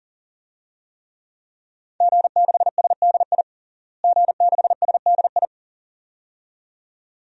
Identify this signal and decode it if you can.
Morse code, heard as a keyed tone, about 40 wpm, 700 Hz. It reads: G6SDI G6SDI
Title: DXCC-100 Call Signs - 5 Characters - Repeated 2x - 40wpm